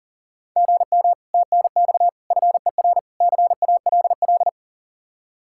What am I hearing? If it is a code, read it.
Morse code, heard as a keyed tone, about 50 wpm, 700 Hz. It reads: GM TNX FER CALL
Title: Speed-Racing Common QSO Phrases Encore - 40wpm